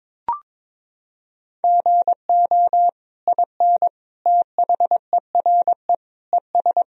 Morse code, heard as a keyed tone, about 21 wpm, 700 Hz. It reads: GO IN THERE ES
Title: Sentences from Top 500 Words and Abbreviations 22wpm